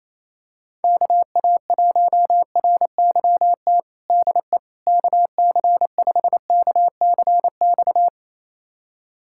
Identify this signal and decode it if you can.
Morse code, heard as a keyed tone, about 28 wpm, 700 Hz. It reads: KA1RYT DE KC5KCX
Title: USA Callsign DE Callsign - 6 Characters - 28wpm